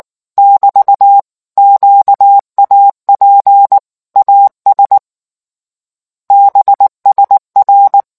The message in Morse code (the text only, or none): <BT> QAP AS BSR